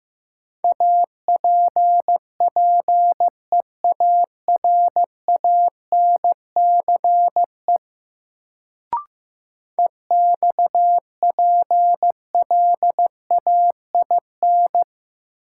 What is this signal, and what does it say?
Morse code, heard as a keyed tone, about 15 wpm, 700 Hz. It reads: APPEARANCE EXPLAIN
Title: Sentences from Top 1100 Words - New Word Drill - 15wpm